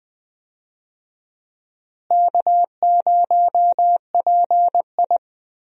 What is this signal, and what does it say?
Morse code, heard as a keyed tone, about 20 wpm, 700 Hz. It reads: K0PI
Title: USA Call Signs - 4 Characters - Repeated 3x - 20wpm